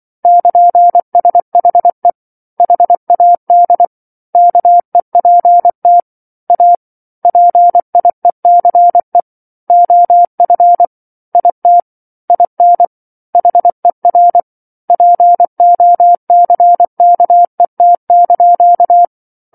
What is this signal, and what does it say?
Morse code, heard as a keyed tone, about 24 wpm, 700 Hz. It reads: <KN>SHE HAD KEPT A PIECE OF IT IN HER POCKET)
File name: 07-Chapter-07 24 WPM